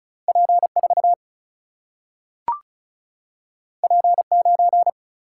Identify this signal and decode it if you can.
Morse code, heard as a keyed tone, about 36 wpm, 700 Hz. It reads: P4 P9
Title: Single-Letter Single-Number 35wpm